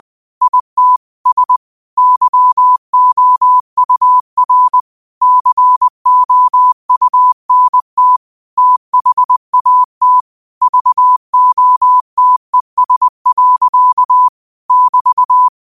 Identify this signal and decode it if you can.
Morse code, heard as a keyed tone, about 20 wpm, 1000 Hz. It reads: IT S YOUR COUNT THAT VOTES. <BT>
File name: 2025-06-25-20wpm